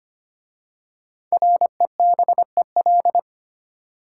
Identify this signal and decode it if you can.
Morse code, heard as a keyed tone, about 25 wpm, 700 Hz. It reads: REBEL
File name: 5 Letter Words - First 3 Letters Given - 25wpm